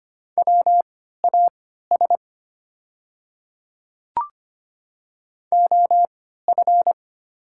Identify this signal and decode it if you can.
Morse code, heard as a keyed tone, about 25 wpm, spaced as for 14 wpm, 690 Hz. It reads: WAS OF
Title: 3x Character Spacing - Top 200 Words - 25wpm